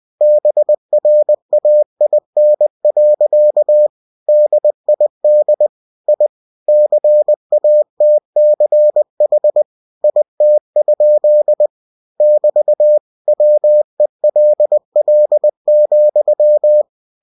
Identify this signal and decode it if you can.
Morse code, heard as a keyed tone, about 20 wpm, 600 Hz. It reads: BRAIN. DID I CATCH IT? <BT> WELL,
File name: cw-books / winnie-the-pooh / winnie-20wpm-page-0011